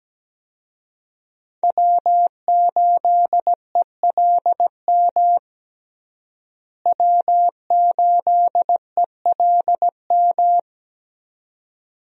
Morse code, heard as a keyed tone, about 17 wpm, 700 Hz. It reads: W8ELM W8ELM
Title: USA Call Signs - 5 Characters - Repeated 2x - 17wpm